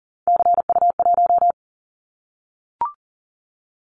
Morse code, heard as a keyed tone, about 38 wpm, 700 Hz. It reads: CU1
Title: Callsign Prefixes for Top 100 DXCC - 40wpm